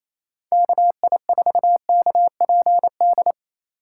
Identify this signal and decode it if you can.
Morse code, heard as a keyed tone, about 28 wpm, 700 Hz. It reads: KI4KPD
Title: USA Call Signs - 6 Characters - 3x Character Spacing - 28wpm